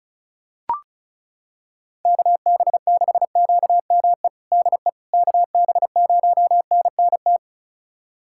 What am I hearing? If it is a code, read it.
Morse code, heard as a keyed tone, about 34 wpm, 700 Hz. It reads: KB6QME DE KB0NNT
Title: USA Callsign DE Callsign - 6 Characters - 35wpm